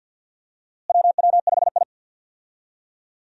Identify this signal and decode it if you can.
Morse code, heard as a keyed tone, about 48 wpm, 700 Hz. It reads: WW5I
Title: USA Call Signs - 4 Characters - Repeated 3x - 50wpm